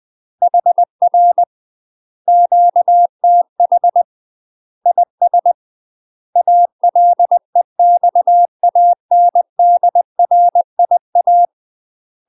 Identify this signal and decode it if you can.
Morse code, heard as a keyed tone, about 20 wpm, 700 Hz. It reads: HR QTH IS ALEXANDRIA
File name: UK MASTER QSOs TRAINING / 20WPM / UK MASTER QSOs 0048